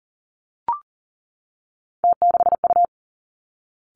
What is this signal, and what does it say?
Morse code, heard as a keyed tone, about 38 wpm, 700 Hz. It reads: T6U
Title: Three Challenging Letter-Number Combinations 40wpm